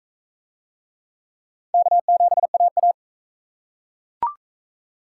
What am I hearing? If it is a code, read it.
Morse code, heard as a keyed tone, about 38 wpm, 700 Hz. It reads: K7AA